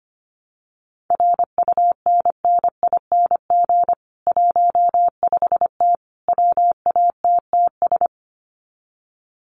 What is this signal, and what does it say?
Morse code, heard as a keyed tone, about 25 wpm, 700 Hz. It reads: RUNNING 15T WATTS
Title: Common QSO Phrases 25wpm